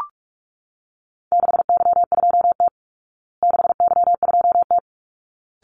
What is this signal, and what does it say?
Morse code, heard as a keyed tone, about 44 wpm, 700 Hz. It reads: 6Y2T 6Y2T
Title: Call Signs Repeated 3x - 45wpm